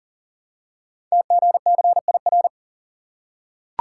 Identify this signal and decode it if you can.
Morse code, heard as a keyed tone, about 39 wpm, 700 Hz. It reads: TGCIR
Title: Code Groups - 5 Letters - 40wpm